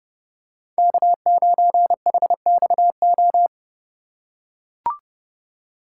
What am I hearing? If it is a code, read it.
Morse code, heard as a keyed tone, about 29 wpm, 690 Hz. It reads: K9HXO
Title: USA Call Signs - 5 Characters - 2x Character Spacing - 30wpm